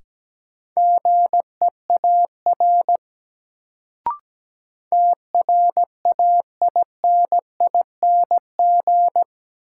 Morse code, heard as a keyed tone, about 17 wpm, 700 Hz. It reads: GEAR TRAINING
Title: Sentences from Top 1400 Words - New Word Drill Alt - 17wpm